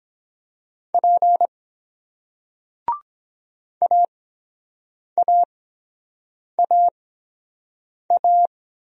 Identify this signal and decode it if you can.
Morse code, heard as a keyed tone, about 23 wpm, spaced as for 7 wpm, 700 Hz. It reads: P AAAA